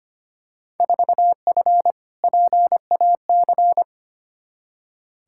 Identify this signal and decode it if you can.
Morse code, heard as a keyed tone, about 25 wpm, 700 Hz. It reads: 4F PAC
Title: Field Day Exchanges Encore - 25wpm